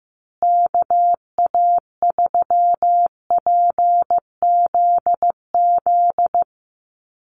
Morse code, heard as a keyed tone, about 15 wpm, 700 Hz. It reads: KA3PZZ
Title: USA Call Signs Repeated 2x 15wpm